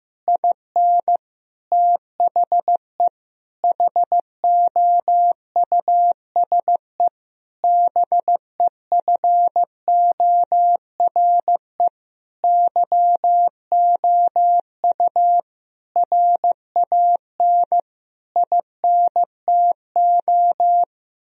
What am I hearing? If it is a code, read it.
Morse code, heard as a keyed tone, about 15 wpm, 700 Hz. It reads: IN THE HOUSE BEFORE YOU RAN INTO
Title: Sentences from Top 1000 Words - Review Part 3 - 15wpm